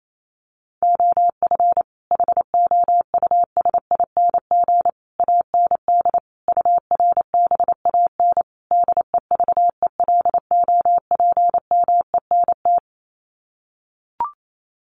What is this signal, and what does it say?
Morse code, heard as a keyed tone, about 28 wpm, 700 Hz. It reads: OF HOUSING AND URBAN DEVELOPMENT
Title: Sets of 5 Words Encore2 28wpm